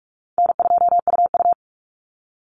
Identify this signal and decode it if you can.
Morse code, heard as a keyed tone, about 44 wpm, 700 Hz. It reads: N2UU
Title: Easy Call Signs - 45wpm